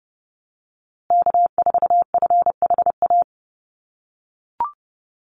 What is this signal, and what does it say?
Morse code, heard as a keyed tone, about 30 wpm, 700 Hz. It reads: K4FHA
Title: USA Call Signs Repeated 2x 30wpm